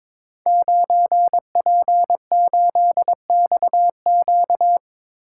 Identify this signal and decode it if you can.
Morse code, heard as a keyed tone, about 23 wpm, 700 Hz. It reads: 9P8XQ
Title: Code Groups - 5 Letter:Numbers - 22wpm